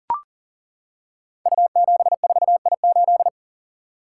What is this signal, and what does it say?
Morse code, heard as a keyed tone, about 39 wpm, 700 Hz. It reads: U74I8
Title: Code Groups - 5 Letter:Numbers - 40wpm